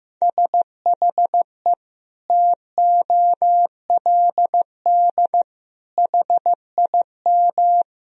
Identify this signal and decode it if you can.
Morse code, heard as a keyed tone, about 15 wpm, 700 Hz. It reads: SHE TOLD HIM
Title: Sentences from Top 1000 Words - Review Part 1 - 15wpm